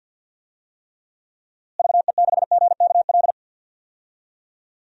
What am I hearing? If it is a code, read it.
Morse code, heard as a keyed tone, about 52 wpm, 700 Hz. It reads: VE6GKL
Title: DXCC-100 Call Signs - 6 Characters - Repeated 2x - 50wpm